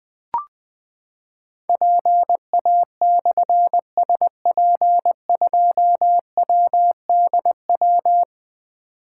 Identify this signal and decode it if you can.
Morse code, heard as a keyed tone, about 20 wpm, 700 Hz. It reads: PA/SP2WDW
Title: Difficult Call Signs 20wpm